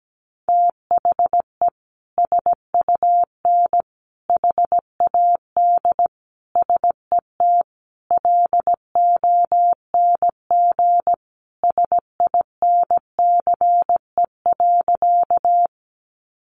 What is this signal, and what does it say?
Morse code, heard as a keyed tone, about 17 wpm, 700 Hz. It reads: THE SUN HAD SET LONG SINCE.
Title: Sentences from Top 300 Words 17wpm